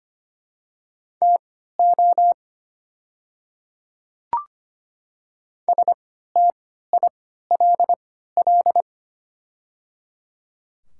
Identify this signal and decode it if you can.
Morse code, heard as a keyed tone, about 25 wpm, spaced as for 14 wpm, 700 Hz. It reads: TO STILL
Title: 3x Character Spacing - Top 500 Words - 25wpm